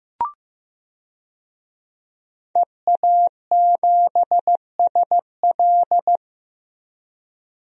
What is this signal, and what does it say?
Morse code, heard as a keyed tone, about 15 wpm, 700 Hz. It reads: EA7SL
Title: DXCC-100 Call Signs - 5 Characters - Repeated 3x - 15wpm